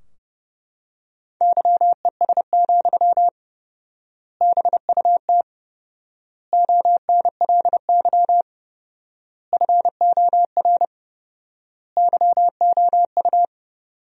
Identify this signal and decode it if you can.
Morse code, heard as a keyed tone, about 30 wpm, 700 Hz. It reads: YES, BUT ONLY FOR YOU
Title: 4x Word-Spacing - Sentences from Top 500 Words - 30wpm